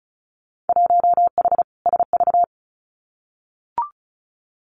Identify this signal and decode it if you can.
Morse code, heard as a keyed tone, about 36 wpm, 700 Hz. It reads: 1H SV
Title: Winter Field Day Exchanges 35wpm